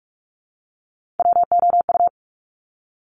Morse code, heard as a keyed tone, about 46 wpm, 700 Hz. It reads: WOU
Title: Top Three Letter Combinations in Words 45wpm